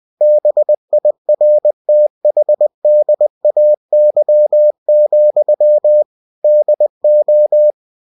Morse code, heard as a keyed tone, about 20 wpm, 600 Hz. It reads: BIRTHDAY, DO
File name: cw-books / winnie-the-pooh / winnie-20wpm-page-0047